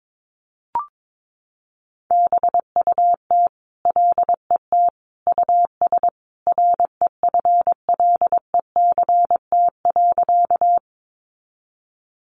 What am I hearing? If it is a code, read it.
Morse code, heard as a keyed tone, about 22 wpm, 700 Hz. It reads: BUT LET US REFLECT.
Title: Sentences from Top 1500 Words 22wpm